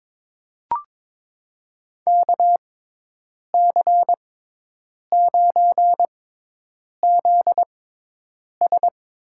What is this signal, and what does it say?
Morse code, heard as a keyed tone, about 22 wpm, spaced as for 8 wpm, 700 Hz. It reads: KC9ZS
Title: USA Call Signs - 6x Character Spacing - 22wpm